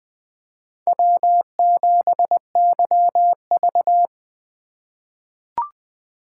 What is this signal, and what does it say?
Morse code, heard as a keyed tone, about 20 wpm, 700 Hz. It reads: W7YV
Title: USA Call Signs - 4 Characters - 2x Character Spacing - 20wpm